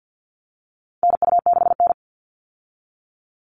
Morse code, heard as a keyed tone, about 50 wpm, 700 Hz. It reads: NU6N
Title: Call Signs Repeated 3x - 50wpm